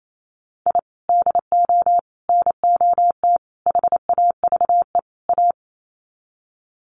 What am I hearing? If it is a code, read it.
Morse code, heard as a keyed tone, about 28 wpm, 700 Hz. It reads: I DO NOT HAVE A
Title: Sets of 5 Words 28wpm